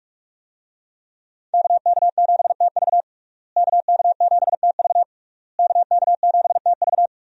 Morse code, heard as a keyed tone, about 46 wpm, 700 Hz. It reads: KK7TV KK7TV KK7TV
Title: USA Call Signs - 5 Characters - Repeated 3x - 45wpm